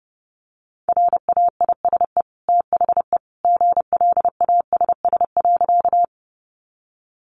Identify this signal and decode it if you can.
Morse code, heard as a keyed tone, about 30 wpm, 700 Hz. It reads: RAISE THE GLASS.
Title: Sentences from Top 1400 Words 30wpm